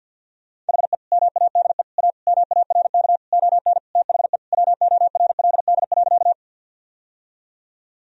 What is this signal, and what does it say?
Morse code, heard as a keyed tone, about 50 wpm, 700 Hz. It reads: HE MADE A MARK ON THE WORLD.
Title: Sentences from Top 1000 Words - Review Part 1 - 50wpm